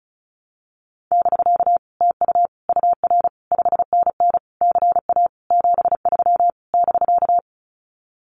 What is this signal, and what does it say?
Morse code, heard as a keyed tone, about 36 wpm, 700 Hz. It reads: <BK> TU UR 5NN CA 73 <BK>
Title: Common QSO Phrases Encore 35wpm